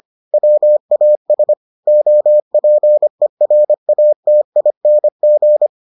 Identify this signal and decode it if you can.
Morse code, heard as a keyed tone, about 25 wpm, 600 Hz. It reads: WAS OPERATING